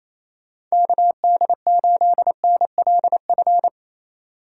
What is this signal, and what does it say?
Morse code, heard as a keyed tone, about 34 wpm, 700 Hz. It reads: KD8NLF
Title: Speed Racing - USA 2x3 Call Signs 28wpm